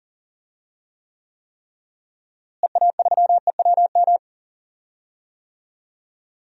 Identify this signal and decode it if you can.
Morse code, heard as a keyed tone, about 40 wpm, 700 Hz. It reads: EA3EWM
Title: DXCC-100 Call Signs - 6 Characters - Repeated 3x - 40wpm